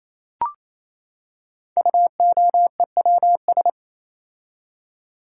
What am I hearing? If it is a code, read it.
Morse code, heard as a keyed tone, about 28 wpm, 700 Hz. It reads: UOEWS
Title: Code Groups - 5 Letters - 28wpm